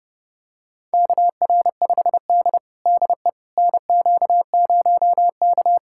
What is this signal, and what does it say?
Morse code, heard as a keyed tone, about 30 wpm, 700 Hz. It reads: KR5D DE NQ0K
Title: USA Callsign DE Callsign - 4 Characters - 30wpm